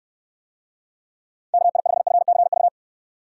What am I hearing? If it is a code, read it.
Morse code, heard as a keyed tone, about 68 wpm, 700 Hz. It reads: KI5FCV